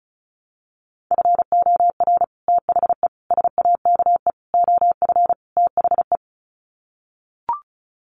Morse code, heard as a keyed tone, about 34 wpm, 700 Hz. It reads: FOR THE SAKE OF THE
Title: Sets of 5 Words Encore 35wpm